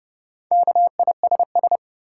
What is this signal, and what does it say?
Morse code, heard as a keyed tone, about 30 wpm, 700 Hz. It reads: KISS
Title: Four Letter Words - 30wpm